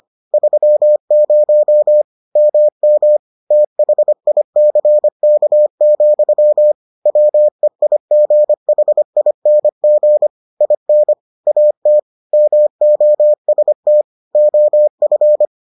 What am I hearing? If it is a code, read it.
Morse code, heard as a keyed tone, about 25 wpm, 600 Hz. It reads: 30 MM THICK, WEIGHING IN AT MOST OF